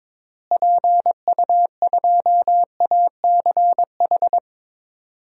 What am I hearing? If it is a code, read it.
Morse code, heard as a keyed tone, about 22 wpm, 700 Hz. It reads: PU2ACH
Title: DXCC-100 Call Signs - 6 Characters - Repeated 3x - 22wpm